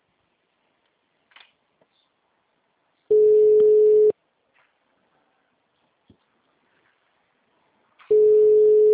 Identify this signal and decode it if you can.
Ringback tone (Europe, 425 Hz).